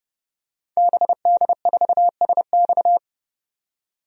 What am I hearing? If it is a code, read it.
Morse code, heard as a keyed tone, about 30 wpm, 700 Hz. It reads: BD4SX
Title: DXCC-100 Call Signs - 5 Characters - 30wpm